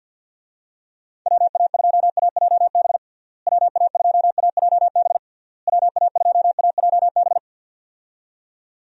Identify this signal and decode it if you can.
Morse code, heard as a keyed tone, about 52 wpm, 700 Hz. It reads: WA2AJB WA2AJB WA2AJB
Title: USA Call Signs - 6 Characters - Repeated 3x - 50wpm